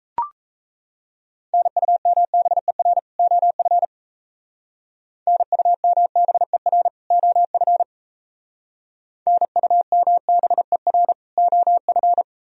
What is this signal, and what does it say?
Morse code, heard as a keyed tone, about 34 wpm, 700 Hz. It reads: NUMBER OF NUMBER OF NUMBER OF
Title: Speed-Racing - Sets of 2 Words 28wpm